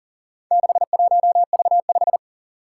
Morse code, heard as a keyed tone, about 39 wpm, 700 Hz. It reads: 61V5
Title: Four Challenging Letter-Number Combinations 40wpm